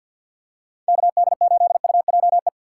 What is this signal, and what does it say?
Morse code, heard as a keyed tone, about 47 wpm, 700 Hz. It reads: KD8UJE
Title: USA Call Signs - 6 Characters - 6x Character Spacing - 50wpm